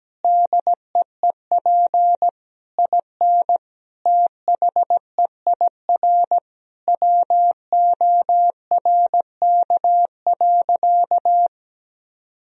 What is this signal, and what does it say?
Morse code, heard as a keyed tone, about 17 wpm, 700 Hz. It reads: DEEP IN THEIR WORK.
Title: Sentences from Top 1000 Words - Review Part 3 - 17wpm